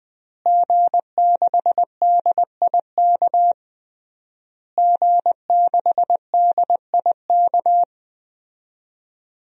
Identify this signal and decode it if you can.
Morse code, heard as a keyed tone, about 20 wpm, 700 Hz. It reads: G6DIK G6DIK
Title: DXCC-100 Call Signs - 5 Characters - Repeated 3x - 20wpm